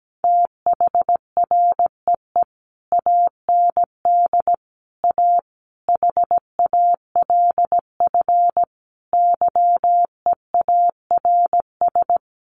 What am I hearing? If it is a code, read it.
Morse code, heard as a keyed tone, about 17 wpm, 700 Hz. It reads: THREE AND A HALF YEARS
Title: Sets of 5 Words Encore2 17wpm